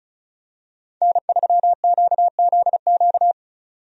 Morse code, heard as a keyed tone, about 34 wpm, 700 Hz. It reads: N3QZQ
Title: USA Call Signs - 5 Characters - 35wpm